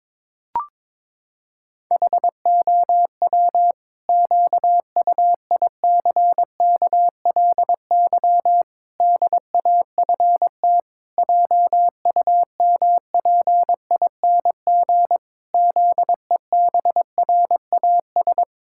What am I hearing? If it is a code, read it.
Morse code, heard as a keyed tone, about 21 wpm, 700 Hz. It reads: HOW QUICKLY DAFT JUMPING ZEBRAS
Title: Pangram Challenge v2 - 22wpm